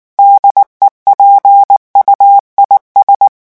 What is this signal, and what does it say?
Morse code, heard as a keyed tone, about 19 wpm, 800 Hz. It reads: DEPUIS
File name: morse mp3 / revision9 citations humoristiques vitesse 18 mots minute